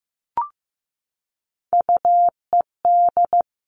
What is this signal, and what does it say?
Morse code, heard as a keyed tone, about 15 wpm, 700 Hz. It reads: UED
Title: All Cut Numbers - 3-Digit 15wpm